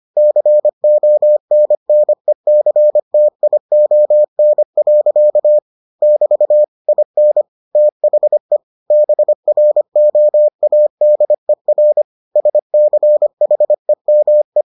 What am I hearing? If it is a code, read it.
Morse code, heard as a keyed tone, about 25 wpm, 600 Hz. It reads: CONNECTION. <BT> IN THE BROADER SCHEME